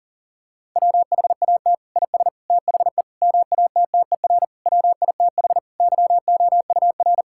Morse code, heard as a keyed tone, about 40 wpm, 700 Hz. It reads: WHAT IS THE MATTER WITH YOUR
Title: Sentences from Top 1000 Words - Review Part 5 - 40wpm